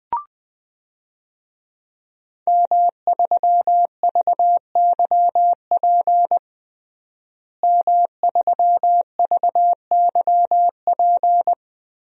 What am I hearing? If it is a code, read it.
Morse code, heard as a keyed tone, about 21 wpm, 700 Hz. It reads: M3VYP M3VYP
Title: DXCC-100 Call Signs - 5 Characters - Repeated 3x - 20wpm